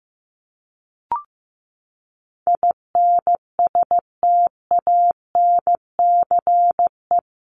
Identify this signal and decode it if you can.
Morse code, heard as a keyed tone, about 15 wpm, 700 Hz. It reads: INSTANCE
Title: Top 501-750 Words 15wpm